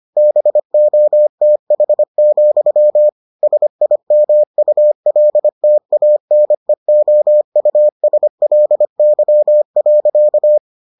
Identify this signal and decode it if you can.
Morse code, heard as a keyed tone, about 25 wpm, 600 Hz. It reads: BOTH, SIMULTANEOUSLY.